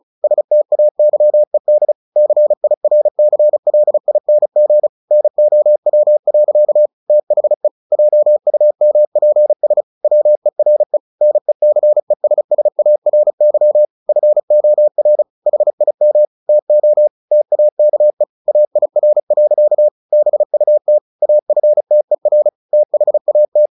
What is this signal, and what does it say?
Morse code, heard as a keyed tone, about 34 wpm, 600 Hz. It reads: STAYED CIRCLING NOW. THE JUMPS WERE NECESSARY FOR HIM TO TAKE AIR. BUT AFTER THAT
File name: the old man and the sea 2 35wpm